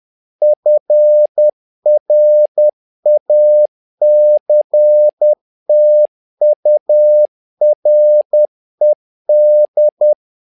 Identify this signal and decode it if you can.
Morse code, heard as a keyed tone, about 10 wpm, 600 Hz. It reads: FRACTURED
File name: learning center / tower trouble / tower trouble 10wpm